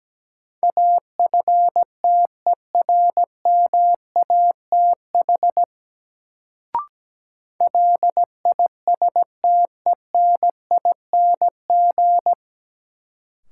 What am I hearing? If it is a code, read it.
Morse code, heard as a keyed tone, about 17 wpm, 700 Hz. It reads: AFTERMATH LISTENING
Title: Sets of 5 Words - New Word Drill - 17wpm